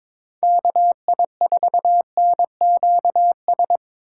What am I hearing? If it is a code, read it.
Morse code, heard as a keyed tone, about 22 wpm, 700 Hz. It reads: KI4NQS
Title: USA Call Signs - 3x Character Spacing - 22wpm